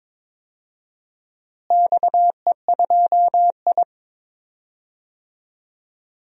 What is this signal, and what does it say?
Morse code, heard as a keyed tone, about 22 wpm, 700 Hz. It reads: XE2I
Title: DXCC-100 Call Signs - 4 Characters - Repeated 2x - 22wpm